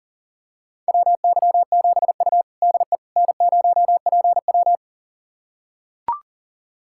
Morse code, heard as a keyed tone, about 40 wpm, 700 Hz. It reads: WY7U DE N0PW
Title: USA Callsign DE Callsign - 4 Characters - 40wpm